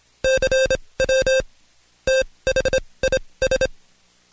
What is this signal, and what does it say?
Morse code, heard as a keyed tone, about 26 wpm, spaced as for 20 wpm, 530 Hz. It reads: CW THIS
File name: Code Quick / AudioDownld2 / Seg136-Stepper9 19WPM